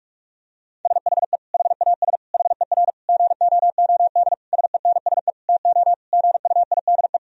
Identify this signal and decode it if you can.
Morse code, heard as a keyed tone, about 42 wpm, 700 Hz. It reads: SHE HAS HER GOOD SENSE TO GUIDE
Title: Sentences from Top 1000 Words - Review Part 4 - 45wpm